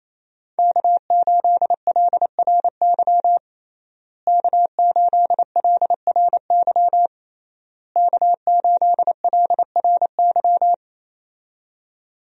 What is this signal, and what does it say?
Morse code, heard as a keyed tone, about 27 wpm, 700 Hz. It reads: K8LRY K8LRY K8LRY
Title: USA Call Signs - 5 Characters - Repeated 3x - 28wpm